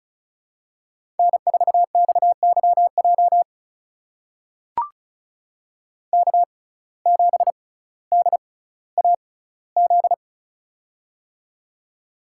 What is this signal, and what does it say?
Morse code, heard as a keyed tone, about 36 wpm, spaced as for 18 wpm, 700 Hz. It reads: N4XYJ K7DAZ
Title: USA Call Signs - 5 Characters - 6x Character Spacing - 35wpm